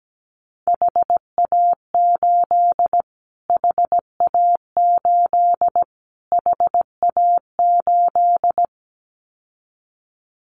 Morse code, heard as a keyed tone, about 17 wpm, 700 Hz. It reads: HA8 HA8 HA8
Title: Callsign Prefixes for Top 25 DXCC - Repeated 3x - 17wpm